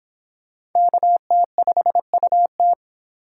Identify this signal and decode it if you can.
Morse code, heard as a keyed tone, about 26 wpm, 700 Hz. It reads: KT5UT